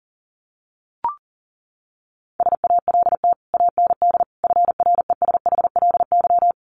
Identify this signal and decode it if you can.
Morse code, heard as a keyed tone, about 40 wpm, 700 Hz. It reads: SALT AND FRESHLY
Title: Sets of 5 Words Encore 40wpm